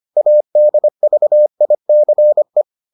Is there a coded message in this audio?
Yes: Morse code from 0.2 to 2.6 s, heard as a keyed tone, about 25 wpm, 600 Hz. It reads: ADVICE